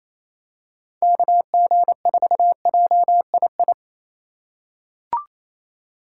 Morse code, heard as a keyed tone, about 28 wpm, 700 Hz. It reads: KG4JII
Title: USA Call Signs - 6 Characters - 4x Character Spacing - 28wpm